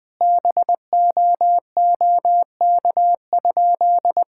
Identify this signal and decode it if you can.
Morse code, heard as a keyed tone, about 19 wpm, 700 Hz. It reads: BOOK?
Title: Sentences from Top 1000 Words - Review Part 5 - 20wpm